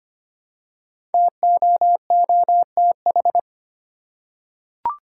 Morse code, heard as a keyed tone, about 25 wpm, 700 Hz. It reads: TOOTH
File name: 5 Letter Words - First 3 Letters Given - 25wpm